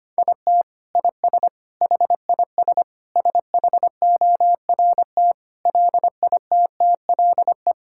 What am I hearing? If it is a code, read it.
Morse code, heard as a keyed tone, about 25 wpm, 700 Hz. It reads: IT IS HIS SHORT LITTLE
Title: Sentences from Top 1600 Words 25wpm